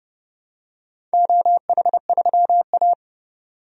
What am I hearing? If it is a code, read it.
Morse code, heard as a keyed tone, about 32 wpm, 700 Hz. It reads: OH3A
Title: DXCC-100 Call Signs - 4 Characters - 30wpm